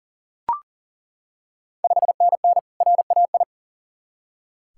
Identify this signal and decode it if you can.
Morse code, heard as a keyed tone, about 40 wpm, 700 Hz. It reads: 5NN RAI
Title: Ontario QSO Party Exchange 40wpm